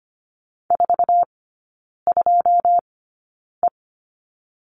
Morse code, heard as a keyed tone, about 25 wpm, spaced as for 9 wpm, 700 Hz. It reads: 42E
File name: course / 25:10wpm / 093 Lesson 16e - Letter c - Three Characters 25wpm